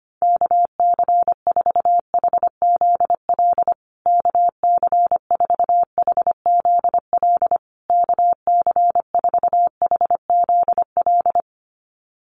Morse code, heard as a keyed tone, about 25 wpm, 700 Hz. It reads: KC4HZL KC4HZL KC4HZL
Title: USA Call Signs Repeated 3x 25wpm